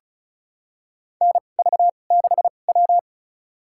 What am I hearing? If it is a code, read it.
Morse code, heard as a keyed tone, about 34 wpm, spaced as for 25 wpm, 700 Hz. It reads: NV6W
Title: USA Call Signs - 4 Characters - 2x Character Spacing - 35wpm